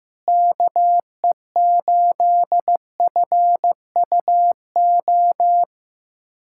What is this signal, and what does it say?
Morse code, heard as a keyed tone, about 15 wpm, 700 Hz. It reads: KE8FUO
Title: USA Callsign DE Callsign - 6 Characters - 15wpm